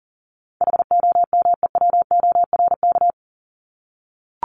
Morse code, heard as a keyed tone, about 39 wpm, 700 Hz. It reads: HOMEWORK
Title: Top 1501-1750 Words 40wpm